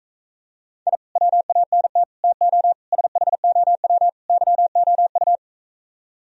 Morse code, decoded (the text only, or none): I WANT TO SHOW YOU